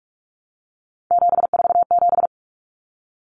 Morse code, heard as a keyed tone, about 44 wpm, 700 Hz. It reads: 747
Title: Common Cut Numbers - 3-Digit 45wpm